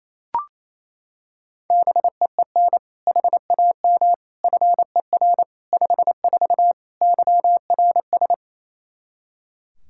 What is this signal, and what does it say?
Morse code, heard as a keyed tone, about 28 wpm, 700 Hz. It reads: BEEN HAM FER 54 YRS
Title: Practice QSO - Exchange 3 - More Info - 28wpm